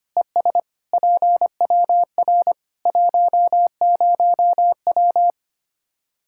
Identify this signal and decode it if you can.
Morse code, heard as a keyed tone, about 25 wpm, 700 Hz. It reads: ES PWR 10W
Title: Practice QSO - Exchange 2 - Info - 25wpm